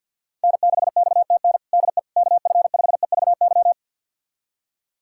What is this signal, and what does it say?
Morse code, heard as a keyed tone, about 47 wpm, 700 Hz. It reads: N6XTN DE KF5EVY
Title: USA Callsign DE Callsign 50wpm